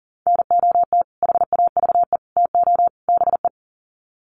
Morse code, heard as a keyed tone, about 40 wpm, 700 Hz. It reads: NOT HAVE TO BE
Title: Sets of 4 Words 40wpm